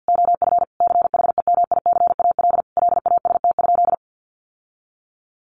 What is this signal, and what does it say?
Morse code, heard as a keyed tone, about 48 wpm, 700 Hz. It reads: OF CHEMICAL LAST?
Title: Sentences from Top 1200 Words 50wpm